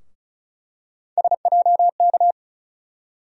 Morse code, heard as a keyed tone, about 34 wpm, 700 Hz. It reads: SJK
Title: USA Call Sign Postfixes - 3 Letters 35wpm